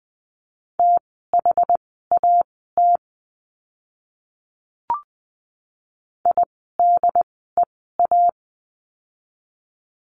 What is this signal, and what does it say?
Morse code, heard as a keyed tone, about 20 wpm, spaced as for 14 wpm, 700 Hz. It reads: THAT IDEA
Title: 108 Lesson 18f - Letter m - Words 20wpm